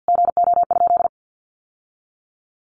Morse code, heard as a keyed tone, about 48 wpm, 700 Hz. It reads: GO?
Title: Sentences from Top 1100 Words 50wpm